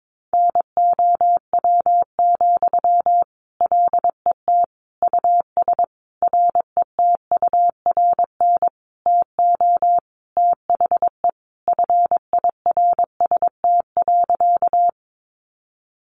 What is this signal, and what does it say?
Morse code, heard as a keyed tone, about 22 wpm, 700 Hz. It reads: NOW, LET US RETURN TO THE FIRST.
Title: Sentences from Top 800 Words 22wpm